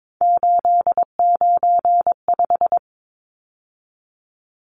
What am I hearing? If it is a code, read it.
Morse code, heard as a keyed tone, about 22 wpm, 700 Hz. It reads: 895
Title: Numbers - 3-Digit 22wpm